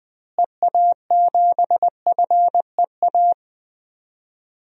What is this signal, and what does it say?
Morse code, heard as a keyed tone, about 20 wpm, 700 Hz. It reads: EA7FEA
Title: DXCC-100 Call Signs - 6 Characters - Repeated 3x - 20wpm